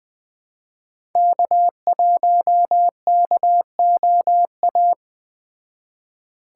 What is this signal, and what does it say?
Morse code, heard as a keyed tone, about 20 wpm, 700 Hz. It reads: K1KOA
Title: Speed Racing - USA 1x3 Call Signs 17wpm